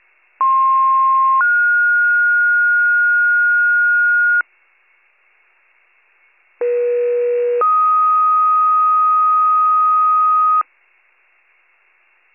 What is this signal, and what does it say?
Two-tone page, heard as a one-second tone, then a three-second tone, about 1060 then 1430 Hz, about 480 then 1250 Hz. It is heard from 0.4 to 10.6 s.